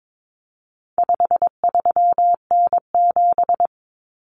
Morse code, heard as a keyed tone, about 22 wpm, 700 Hz. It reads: 53N7
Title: Common Cut Numbers - 4-Digit 22wpm